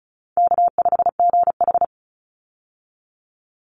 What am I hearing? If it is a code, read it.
Morse code, heard as a keyed tone, about 34 wpm, 700 Hz. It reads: K5GH
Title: Easy Call Signs Repeated 2x - 35wpm